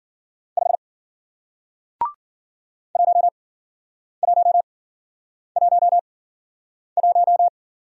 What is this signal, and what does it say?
Morse code, heard as a keyed tone, about 53 wpm, spaced as for 9 wpm, 700 Hz. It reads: 5 1111